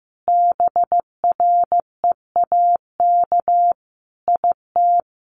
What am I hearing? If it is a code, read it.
Morse code, heard as a keyed tone, about 15 wpm, 700 Hz. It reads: BREAK IT
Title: Sentences from Top 900 Words 15wpm